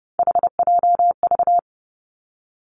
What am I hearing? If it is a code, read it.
Morse code, heard as a keyed tone, about 30 wpm, 700 Hz. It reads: HJV
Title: Three Challenging Letter-Number Combinations 30wpm